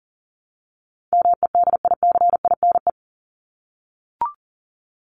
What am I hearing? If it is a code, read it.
Morse code, heard as a keyed tone, about 40 wpm, 700 Hz. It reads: MEDICINE
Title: Top 751-1000 Words 40wpm